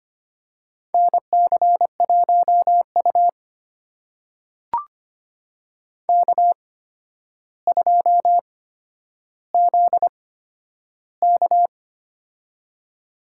Morse code, heard as a keyed tone, about 25 wpm, spaced as for 11 wpm, 700 Hz. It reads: NC1U K2ZK
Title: USA Call Signs - 4 Characters - 8x Character Spacing - 25wpm